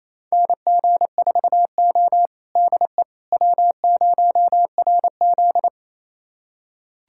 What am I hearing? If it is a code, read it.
Morse code, heard as a keyed tone, about 28 wpm, 700 Hz. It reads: NG4O DE W0RZ
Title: USA Callsign DE Callsign - 4 Characters - 28wpm